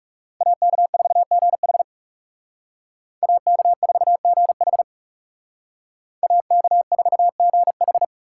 Morse code, heard as a keyed tone, about 37 wpm, 700 Hz. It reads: AK4GH AK4GH AK4GH
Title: Speed Racing - USA 2x2 Call Signs 30wpm